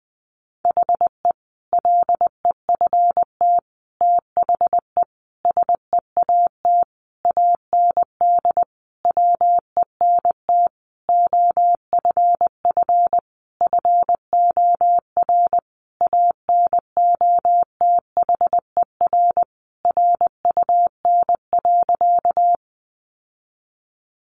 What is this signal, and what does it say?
Morse code, heard as a keyed tone, about 20 wpm, 700 Hz. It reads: HE LEFT THE SEAT AND WENT OFF FOR ANOTHER RUN.
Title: Sentences from Top 1500 Words 20wpm